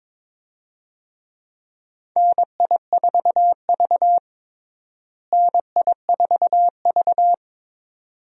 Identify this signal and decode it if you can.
Morse code, heard as a keyed tone, about 21 wpm, 700 Hz. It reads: NI4V NI4V
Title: USA Call Signs - 4 Characters - Repeated 2x - 22wpm